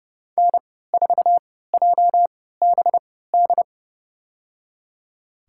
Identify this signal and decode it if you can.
Morse code, heard as a keyed tone, about 30 wpm, spaced as for 17 wpm, 700 Hz. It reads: N4JBD
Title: USA Call Signs - 3x Character Spacing - 30wpm